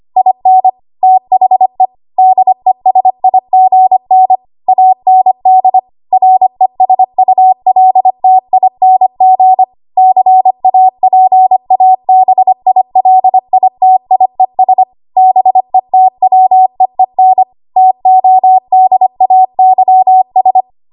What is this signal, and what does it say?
Morse code, heard as a keyed tone, about 25 wpm, 750 Hz. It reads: IN THE DESIGN AND RESULTING CAPABILITIES BETWEEN TODAYS
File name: Archive / 25 WPM / 25WPM